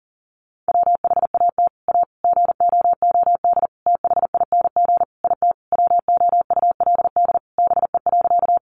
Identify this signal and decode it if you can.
Morse code, heard as a keyed tone, about 40 wpm, 700 Hz. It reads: WHAT A GOOD THING IT WOULD BE.
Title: Sentences from Top 200 Words 40wpm